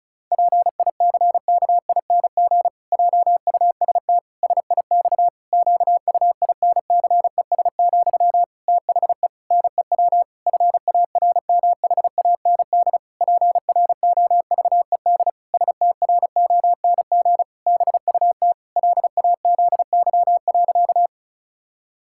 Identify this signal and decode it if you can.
Morse code, heard as a keyed tone, about 36 wpm, 700 Hz. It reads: PICKING JUST SIX QUINCES, THE NEW FARMHAND PROVED STRONG BUT LAZY.
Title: Pangram Challenge v2 - 35wpm